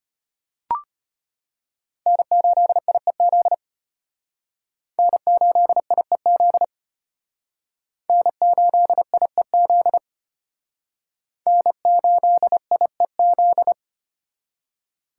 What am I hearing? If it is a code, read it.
Morse code, heard as a keyed tone, about 32 wpm, 700 Hz. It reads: N8IEZ N8IEZ N8IEZ N8IEZ